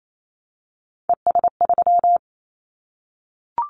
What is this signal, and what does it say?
Morse code, heard as a keyed tone, about 28 wpm, 700 Hz. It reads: ES3
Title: Callsign Prefixes for Top 100 DXCC - 28wpm